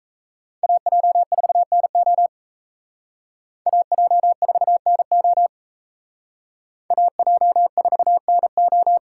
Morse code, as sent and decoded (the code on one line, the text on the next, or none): .- .--- ....- -. --- / .- .--- ....- -. --- / .- .--- ....- -. ---
AJ4NO AJ4NO AJ4NO